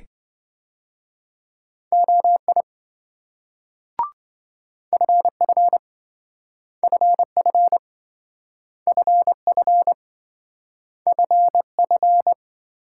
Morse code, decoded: OI FF FF FF FF

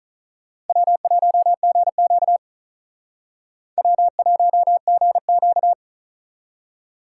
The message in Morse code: W1GQ W1GQ

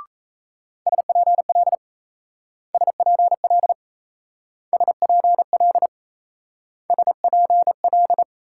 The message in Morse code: SPL SPL SPL SPL

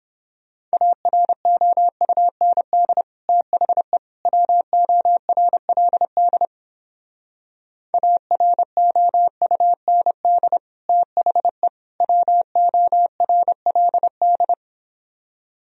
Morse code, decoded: AROUND THE WORLD AROUND THE WORLD

